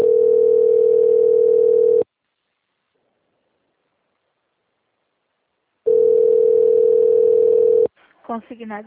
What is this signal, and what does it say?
Ringback tone (North America), 440 + 480 Hz, with speech.
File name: out- - - - - .17422